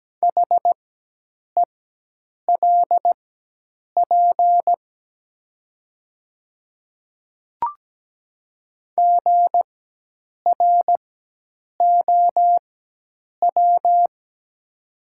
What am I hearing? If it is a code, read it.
Morse code, heard as a keyed tone, about 17 wpm, spaced as for 8 wpm, 690 Hz. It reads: HELP GROW
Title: 4x Character Spacing - Top 300 Words - 17wpm